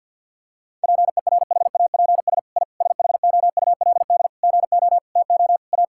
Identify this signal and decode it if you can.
Morse code, heard as a keyed tone, about 49 wpm, 700 Hz. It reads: PERHAPS I SHOULD GO TO A
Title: Sentences from Top 1000 Words - Review Part 5 - 50wpm